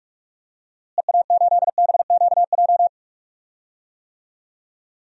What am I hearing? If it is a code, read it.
Morse code, heard as a keyed tone, about 41 wpm, 700 Hz. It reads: EA8BQJ
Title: DXCC-100 Call Signs - 6 Characters - Repeated 2x - 45wpm